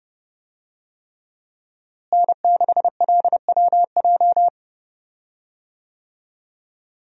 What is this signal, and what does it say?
Morse code, heard as a keyed tone, about 31 wpm, 700 Hz. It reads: N6LWJ
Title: USA Call Signs - 5 Characters - Repeated 3x - 30wpm